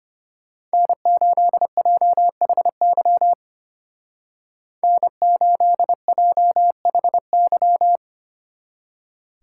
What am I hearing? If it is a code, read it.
Morse code, heard as a keyed tone, about 31 wpm, 700 Hz. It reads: N8JHY N8JHY
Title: Speed Racing - USA 1x3 Call Signs 25wpm